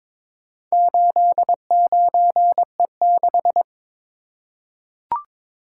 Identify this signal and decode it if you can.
Morse code, heard as a keyed tone, about 22 wpm, 700 Hz. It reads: 89E6
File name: practice / 22wpm / Four Letter-Number Combinations - 22wpm